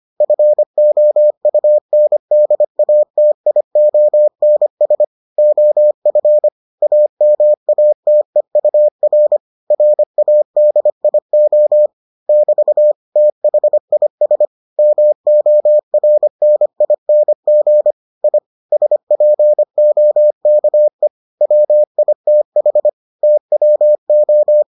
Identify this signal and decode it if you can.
Morse code, heard as a keyed tone, about 25 wpm, 600 Hz. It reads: FOUNDATIONS OF AMATEUR RADIO <BT> THIS MORNING I SPOKE WITH TWO